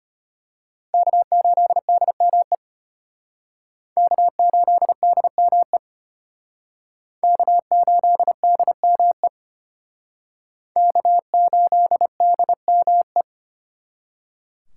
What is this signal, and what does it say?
Morse code, heard as a keyed tone, about 34 wpm, 700 Hz. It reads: K8DME K8DME K8DME K8DME